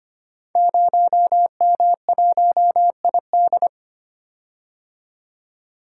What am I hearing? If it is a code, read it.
Morse code, heard as a keyed tone, about 25 wpm, 700 Hz. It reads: 0M1ID
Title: Code Groups - 5 Letter:Numbers - 25wpm